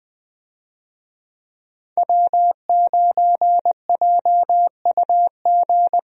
Morse code, heard as a keyed tone, about 21 wpm, 700 Hz. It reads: W9JUG
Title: USA Call Signs - 5 Characters - Repeated 3x - 20wpm